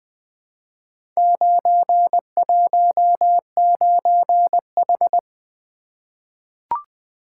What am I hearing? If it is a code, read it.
Morse code, heard as a keyed tone, about 20 wpm, 700 Hz. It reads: 919H
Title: Four Letter-Number Combinations - 20wpm